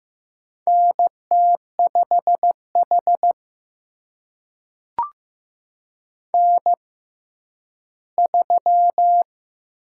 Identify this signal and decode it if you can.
Morse code, heard as a keyed tone, about 15 wpm, spaced as for 10 wpm, 700 Hz. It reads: NT5H N3
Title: USA Call Signs - 4 Characters - 6x Character Spacing - 15wpm